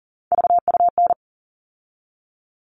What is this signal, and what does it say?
Morse code, heard as a keyed tone, about 40 wpm, 700 Hz. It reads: VUN
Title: All Cut Numbers - 3-Digit 40wpm